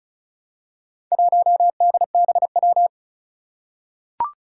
Morse code, heard as a keyed tone, about 36 wpm, 700 Hz. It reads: 1DBW
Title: Four Challenging Letter-Number Combinations 35wpm